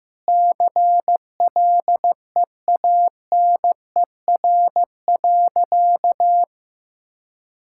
Morse code, heard as a keyed tone, about 15 wpm, 700 Hz. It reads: CLEANER.
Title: Pangram Challenge v2 - 15wpm